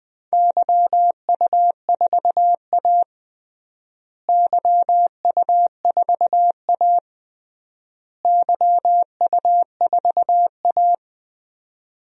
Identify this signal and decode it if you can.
Morse code, heard as a keyed tone, about 20 wpm, 700 Hz. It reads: YU4A YU4A YU4A
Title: DXCC-100 Call Signs - 4 Characters - Repeated 3x - 20wpm